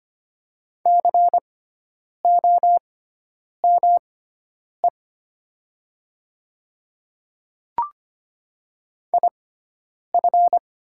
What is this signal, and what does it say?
Morse code, heard as a keyed tone, about 25 wpm, spaced as for 9 wpm, 700 Hz. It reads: COME IF
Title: 6x Character Spacing - Top 200 Words - 25wpm